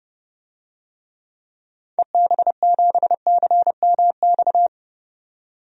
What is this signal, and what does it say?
Morse code, heard as a keyed tone, about 30 wpm, 700 Hz. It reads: EB7CMX
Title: DXCC-100 Call Signs - 6 Characters - 30wpm